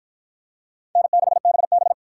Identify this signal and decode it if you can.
Morse code, heard as a keyed tone, about 53 wpm, 700 Hz. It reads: N6BB